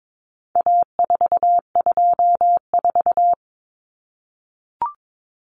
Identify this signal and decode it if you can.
Morse code, heard as a keyed tone, about 22 wpm, 700 Hz. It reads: A424
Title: Common Cut Numbers - 4-Digit 22wpm